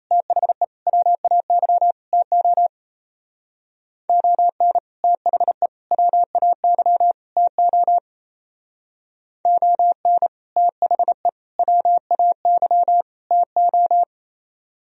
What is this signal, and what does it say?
Morse code, heard as a keyed tone, about 34 wpm, 700 Hz. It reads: THE WAY TO ON THE WAY TO ON THE WAY TO
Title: Speed-Racing - Sets of 4 Words 28wpm